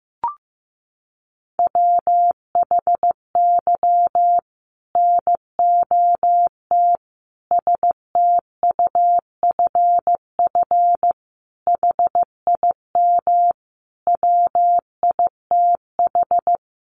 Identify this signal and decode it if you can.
Morse code, heard as a keyed tone, about 15 wpm, 700 Hz. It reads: WHY NOT STUFF HIM WITH
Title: Sentences from Top 1400 Words 15wpm